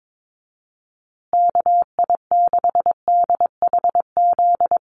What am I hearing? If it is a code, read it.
Morse code, heard as a keyed tone, about 22 wpm, 700 Hz. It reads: KI6DHZ
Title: USA Call Signs Repeated 2x 22wpm